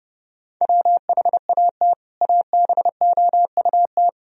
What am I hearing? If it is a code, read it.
Morse code, heard as a keyed tone, about 32 wpm, 700 Hz. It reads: WHAT ABOUT
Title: Sentences from Top 1000 Words - Review Part 1 - 30wpm